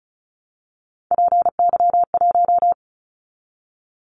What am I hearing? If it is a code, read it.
Morse code, heard as a keyed tone, about 36 wpm, 700 Hz. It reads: PY1
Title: Callsign Prefixes for Top 50 DXCC - 35wpm